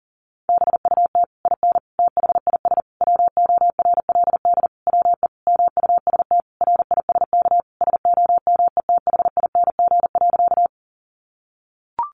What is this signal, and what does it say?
Morse code, heard as a keyed tone, about 40 wpm, 700 Hz. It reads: BUT IN THIS WORLD WE MUST RISK SOMETHING.
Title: Sentences from Top 600 Words 40wpm